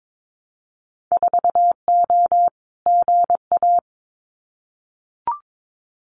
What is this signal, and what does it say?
Morse code, heard as a keyed tone, about 22 wpm, 700 Hz. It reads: 4O GA
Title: Winter Field Day Exchanges 22wpm